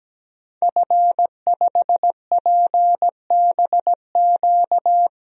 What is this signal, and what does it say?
Morse code, heard as a keyed tone, about 17 wpm, 700 Hz. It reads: F5PBQ
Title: DXCC-100 Call Signs - 5 Characters - 17wpm